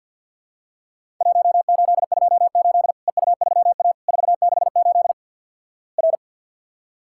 Morse code, heard as a keyed tone, about 52 wpm, 700 Hz. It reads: 1828 EU3A 468 R
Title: Worked All Europe DX Contest - QTC Exchange - 50wpm